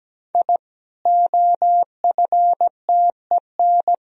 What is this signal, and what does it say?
Morse code, heard as a keyed tone, about 17 wpm, 700 Hz. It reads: I OFTEN
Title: Sentences from Top 1000 Words - Review Part 1 - 17wpm